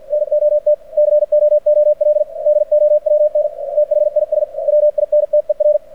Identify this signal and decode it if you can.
Morse code, heard as a keyed tone, about 51 wpm, 600 Hz. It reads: NOT COOL DOWN THIS WINTER
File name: archived / CRSnet